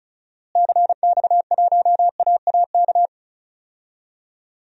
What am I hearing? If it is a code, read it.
Morse code, heard as a keyed tone, about 36 wpm, 700 Hz. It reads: CX1AAK
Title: DXCC-100 Call Signs - 6 Characters - Repeated 2x - 35wpm